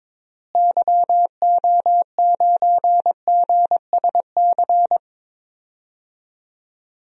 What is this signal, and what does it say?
Morse code, heard as a keyed tone, about 23 wpm, 700 Hz. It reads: YO9GSC
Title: DXCC-100 Call Signs - 6 Characters - Repeated 3x - 22wpm